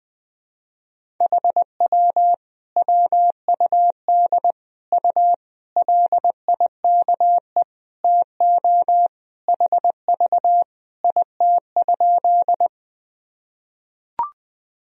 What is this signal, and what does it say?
Morse code, heard as a keyed tone, about 20 wpm, 700 Hz. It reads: HW WUD U LIKE TO HV IT?
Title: Sentences from Top 100 Words and Abbreviations 20wpm